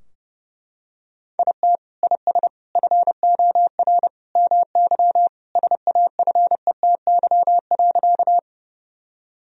Morse code, heard as a keyed tone, about 30 wpm, 700 Hz. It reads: IT IS FOR MY SAFETY.
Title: Sentences from Top 1000 Words - Review Part 4 - 30wpm